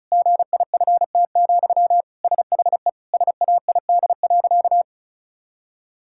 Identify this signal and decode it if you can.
Morse code, heard as a keyed tone, about 34 wpm, 700 Hz. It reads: GIFT, SHE SAID.
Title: Sentences from Top 1000 Words - Review Part 6 - 35wpm